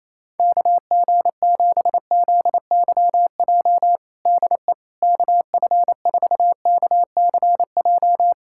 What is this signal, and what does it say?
Morse code, heard as a keyed tone, about 28 wpm, 700 Hz. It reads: KG7ZYJ DE KF4KCJ
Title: USA Callsign DE Callsign - 6 Characters - 28wpm